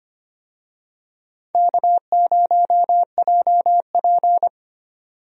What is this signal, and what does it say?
Morse code, heard as a keyed tone, about 25 wpm, 700 Hz. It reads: K0JP
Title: USA Call Signs - 4 Characters - 25wpm